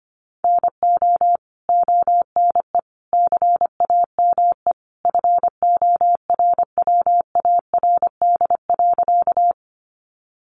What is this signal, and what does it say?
Morse code, heard as a keyed tone, about 25 wpm, 700 Hz. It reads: NO ONE CAME FORWARD.
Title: Sentences from Top 1300 Words 25wpm